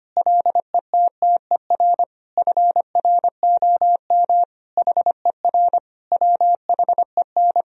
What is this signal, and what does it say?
Morse code, heard as a keyed tone, about 25 wpm, 700 Hz. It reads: LETTER FROM HER WHEN
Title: Sentences from Top 1000 Words - Review Part 3 - 25wpm